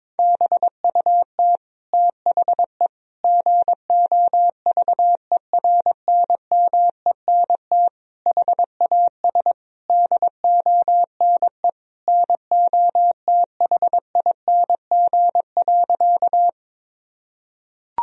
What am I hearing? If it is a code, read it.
Morse code, heard as a keyed tone, about 22 wpm, 700 Hz. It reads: BUT THE GOVERNMENT HAS DONE NOTHING.
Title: Sentences from Top 1000 Words - Review Part 3 - 22wpm